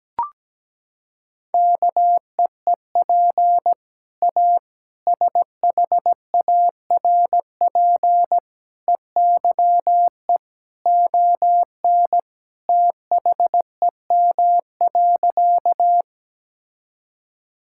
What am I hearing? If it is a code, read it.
Morse code, heard as a keyed tone, about 17 wpm, 700 Hz. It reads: KEEP A SHARP EYE ON THEM.
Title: Sentences from Top 1600 Words 17wpm